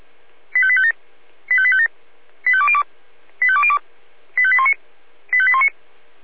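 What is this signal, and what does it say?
Five-tone selective call (ZVEI), heard as quick runs of five tones, tones of about 70 ms. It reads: 85857 85857 85202 85202 85819 85819
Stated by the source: ZVEI five-tone call received by SDR, server of Feuerwehr Orlamünde (fire brigade).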